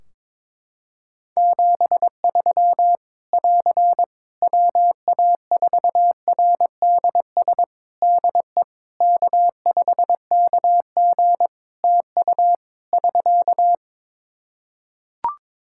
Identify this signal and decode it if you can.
Morse code, heard as a keyed tone, about 22 wpm, 700 Hz. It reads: 73 <AR> WA4RDS DE K5KG TU <SK>
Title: Practice QSO - Exchange 4 - Ending - 22wpm